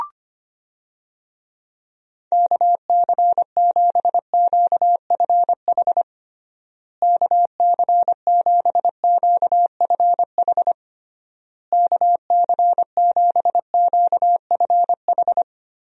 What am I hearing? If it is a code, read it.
Morse code, heard as a keyed tone, about 25 wpm, 700 Hz. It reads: KC7QFH KC7QFH KC7QFH
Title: USA Call Signs - 6 Characters - Repeated 3x - 25wpm